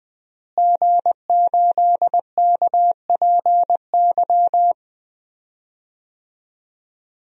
Morse code, heard as a keyed tone, about 20 wpm, 700 Hz. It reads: G8KPY
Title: DXCC-100 Call Signs - 5 Characters - Repeated 2x - 20wpm